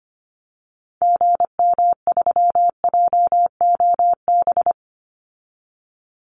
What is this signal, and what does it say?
Morse code, heard as a keyed tone, about 25 wpm, 700 Hz. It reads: GM3JOB
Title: Call Signs Repeated 2x - 25wpm